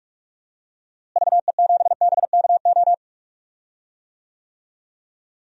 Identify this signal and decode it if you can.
Morse code, heard as a keyed tone, about 43 wpm, 700 Hz. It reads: VE7BKO
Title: DXCC-100 Call Signs - 6 Characters - Repeated 3x - 45wpm